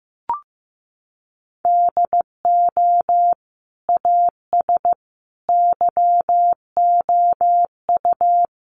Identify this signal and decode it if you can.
Morse code, heard as a keyed tone, about 15 wpm, 700 Hz. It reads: DO AS YOU
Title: Sentences from Top 100 Words 15wpm